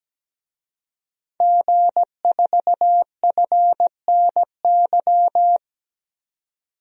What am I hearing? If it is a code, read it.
Morse code, heard as a keyed tone, about 17 wpm, 700 Hz. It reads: G4FNY
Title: DXCC-100 Call Signs - 5 Characters - 17wpm